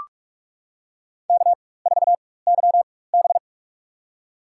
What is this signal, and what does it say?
Morse code, heard as a keyed tone, about 46 wpm, spaced as for 21 wpm, 700 Hz. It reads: K4YB
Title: USA Call Signs - 4 Characters - 4x Character Spacing - 45wpm